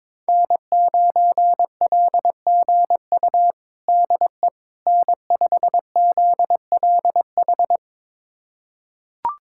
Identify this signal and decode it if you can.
Morse code, heard as a keyed tone, about 22 wpm, 700 Hz. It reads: N9LGU DE N5ZLH
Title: USA Callsign DE Callsign - 5 Characters - 22wpm